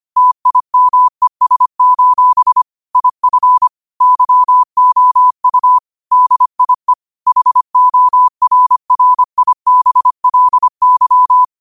Morse code, heard as a keyed tone, about 25 wpm, 1000 Hz. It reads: TIMES: IF YOU DIE HORRIBLY
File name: qotdfiles / 25wpm